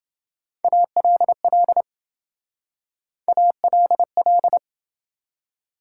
Morse code, heard as a keyed tone, about 27 wpm, 700 Hz. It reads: ALL ALL